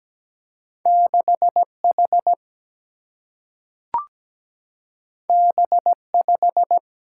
Morse code, heard as a keyed tone, about 17 wpm, 700 Hz. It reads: 6H B5
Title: Single-Letter Single-Number 17wpm